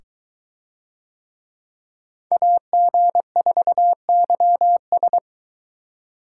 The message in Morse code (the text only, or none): AG4YS